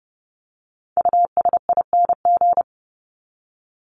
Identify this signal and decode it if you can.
Morse code, heard as a keyed tone, about 30 wpm, 700 Hz. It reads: USING
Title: ING Suffix - Top 100 Words with Base Words 30wpm